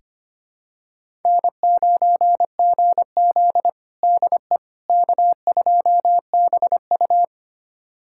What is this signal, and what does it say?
Morse code, heard as a keyed tone, about 25 wpm, 700 Hz. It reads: N9GZ DE K2BU
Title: USA Callsign DE Callsign - 4 Characters - 25wpm